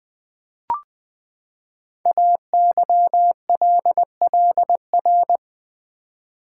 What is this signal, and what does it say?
Morse code, heard as a keyed tone, about 20 wpm, 700 Hz. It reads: AYLLR
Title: Code Groups - 5 Letters - 20wpm